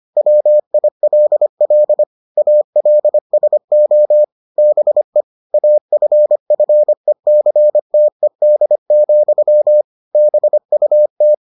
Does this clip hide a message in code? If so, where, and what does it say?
Yes: Morse code from 0.2 to 11.3 s, heard as a keyed tone, about 25 wpm, 600 Hz. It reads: WILL ALSO BE AFFECTED, BUT